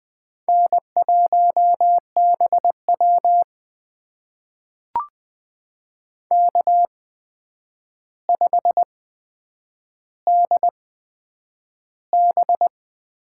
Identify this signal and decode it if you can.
Morse code, heard as a keyed tone, about 20 wpm, spaced as for 9 wpm, 700 Hz. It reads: N1BW K5DB
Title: USA Call Signs - 4 Characters - 8x Character Spacing - 20wpm